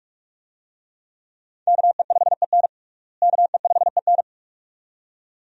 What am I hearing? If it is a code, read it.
Morse code, heard as a keyed tone, about 46 wpm, 700 Hz. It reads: KE5EN KE5EN
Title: USA Call Signs - 5 Characters - Repeated 2x - 45wpm